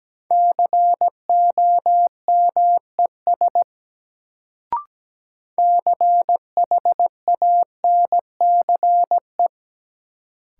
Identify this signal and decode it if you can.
Morse code, heard as a keyed tone, about 17 wpm, 700 Hz. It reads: COMES CHANCE
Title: Sets of 3 Words - New Word Drill - 17wpm